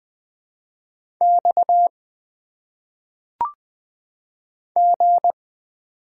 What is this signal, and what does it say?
Morse code, heard as a keyed tone, about 20 wpm, 700 Hz. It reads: X G
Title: Single Letters - Phonetic Pronunciation 20wpm